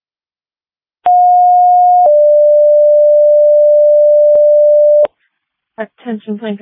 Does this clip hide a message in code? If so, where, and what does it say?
Yes: two-tone page from 1.1 to 5.0 s, heard as a one-second tone, then a three-second tone, about 720 then 590 Hz, with speech.